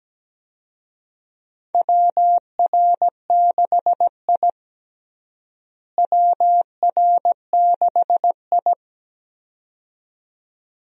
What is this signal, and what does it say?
Morse code, heard as a keyed tone, about 17 wpm, 700 Hz. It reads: WR6I WR6I
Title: USA Call Signs - 4 Characters - Repeated 2x - 17wpm